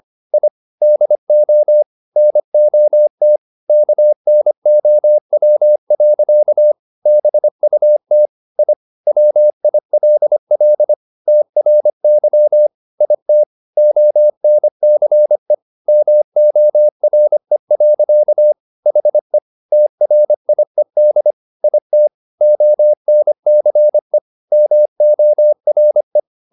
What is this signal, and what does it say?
Morse code, heard as a keyed tone, about 25 wpm, 600 Hz. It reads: I DO NOT KNOW. BUT I WILL TRY IT ONCE MORE. HE TRIED IT ONCE MORE